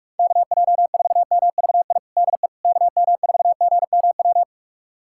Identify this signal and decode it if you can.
Morse code, heard as a keyed tone, about 44 wpm, 690 Hz. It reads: KJ4MVI DE KM4GMW
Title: USA Callsign DE Callsign - 6 Characters - 45wpm